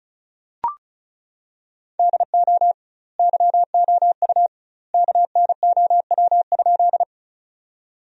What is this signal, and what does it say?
Morse code, heard as a keyed tone, about 36 wpm, 700 Hz. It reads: DO YOU KNOW?
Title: 2x Word-Spacing - Sentences from Top 100 Words - 35wpm